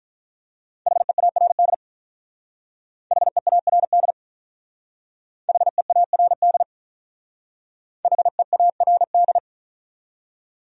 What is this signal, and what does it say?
Morse code, heard as a keyed tone, about 42 wpm, 700 Hz. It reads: HEARD HEARD HEARD HEARD
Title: Speed-Racing Top 500 Words 35wpm